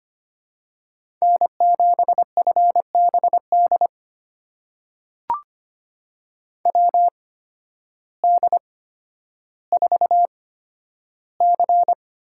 Morse code, heard as a keyed tone, about 25 wpm, spaced as for 12 wpm, 700 Hz. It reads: N7FBD WD4C
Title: USA Call Signs - 8x Character Spacing - 25wpm